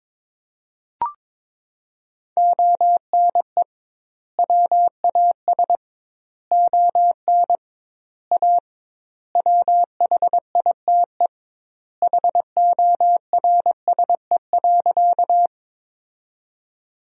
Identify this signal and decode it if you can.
Morse code, heard as a keyed tone, about 22 wpm, 700 Hz. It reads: ONE WAS ON A WHITE HORSE.
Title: 2x Word-Spacing - Sentences from Top 300 Words - 22wpm